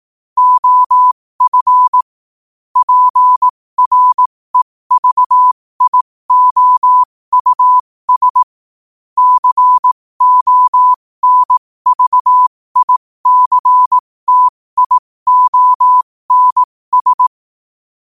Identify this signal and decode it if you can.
Morse code, heard as a keyed tone, about 18 wpm, spaced as for 15 wpm, 1000 Hz. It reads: OF PREVIOUS CONVICTIONS
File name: qotdfiles / 15wpm